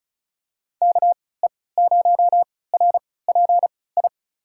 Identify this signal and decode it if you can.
Morse code, heard as a keyed tone, about 34 wpm, spaced as for 20 wpm, 700 Hz. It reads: KE0RPI
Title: USA Call Signs - 6 Characters - 3x Character Spacing - 35wpm